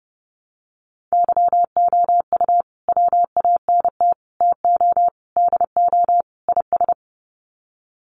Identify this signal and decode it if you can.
Morse code, heard as a keyed tone, about 30 wpm, 700 Hz. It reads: YOU WANT TO DO IS
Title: Sets of 5 Words Encore2 30wpm